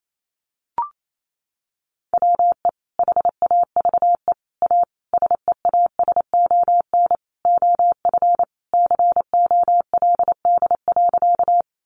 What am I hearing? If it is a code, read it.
Morse code, heard as a keyed tone, about 28 wpm, 700 Hz. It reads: WE HAVE A SEASON OF COLD.
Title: Sentences from Top 1400 Words 28wpm